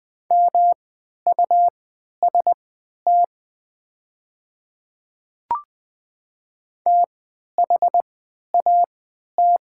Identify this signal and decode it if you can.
Morse code, heard as a keyed tone, about 20 wpm, spaced as for 11 wpm, 690 Hz. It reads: MUST THAT
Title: 3x Character Spacing - Top 300 Words - 20wpm